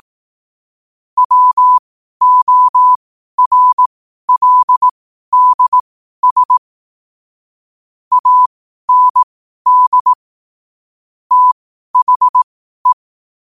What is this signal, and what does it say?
Morse code, heard as a keyed tone, about 18 wpm, spaced as for 12 wpm, 1000 Hz. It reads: WORLDS AND THE
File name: qotdfiles / 12wpm